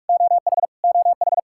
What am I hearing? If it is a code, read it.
Morse code, heard as a keyed tone, about 46 wpm, 700 Hz. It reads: OH OH
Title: Practice QSO - Exchange 1 - Beginning - 45wpm